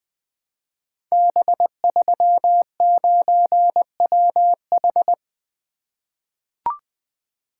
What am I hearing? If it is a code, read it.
Morse code, heard as a keyed tone, about 20 wpm, 700 Hz. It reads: B39WH
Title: Code Groups - 5 Letter:Numbers - 20wpm